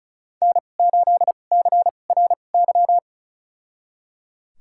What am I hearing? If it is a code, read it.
Morse code, heard as a keyed tone, about 34 wpm, spaced as for 25 wpm, 700 Hz. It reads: N8CRY
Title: USA Call Signs - 5 Characters - 2x Character Spacing - 35wpm